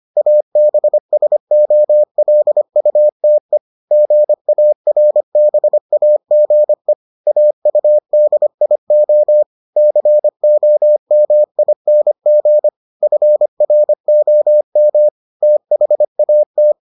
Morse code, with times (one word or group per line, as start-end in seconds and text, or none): 0.2-3.6 ABSOLUTE
3.9-6.9 GARBAGE
7.3-9.4 AUDIO
9.8-12.7 COMING
13.0-15.1 FROM
15.4-16.7 THAT